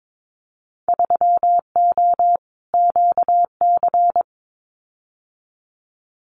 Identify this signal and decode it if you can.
Morse code, heard as a keyed tone, about 22 wpm, 700 Hz. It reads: 3O QC
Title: Winter Field Day Exchanges Encore 22wpm